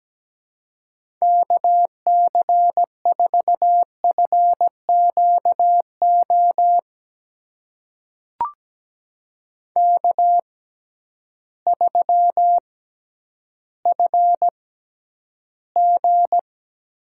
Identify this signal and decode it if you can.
Morse code, heard as a keyed tone, about 17 wpm, spaced as for 10 wpm, 700 Hz. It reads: KC4FQO K3FG
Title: USA Call Signs - 6x Character Spacing - 17wpm